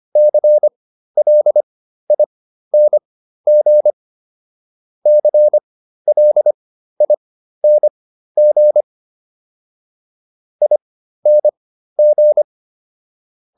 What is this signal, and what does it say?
Morse code, heard as a keyed tone, about 25 wpm, spaced as for 13 wpm, 600 Hz. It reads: CLING CLING ING